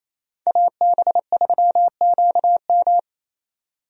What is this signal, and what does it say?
Morse code, heard as a keyed tone, about 28 wpm, 700 Hz. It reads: AB3QM
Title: USA Call Signs - 5 Characters - 28wpm